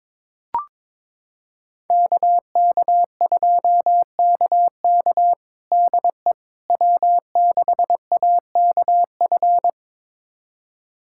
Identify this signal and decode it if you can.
Morse code, heard as a keyed tone, about 22 wpm, 700 Hz. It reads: KK2KK DE W6AKF
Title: USA Callsign DE Callsign - 5 Characters - 22wpm